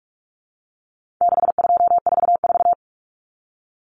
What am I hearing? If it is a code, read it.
Morse code, heard as a keyed tone, about 44 wpm, 700 Hz. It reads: 6244
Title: Numbers - 4-Digit 45wpm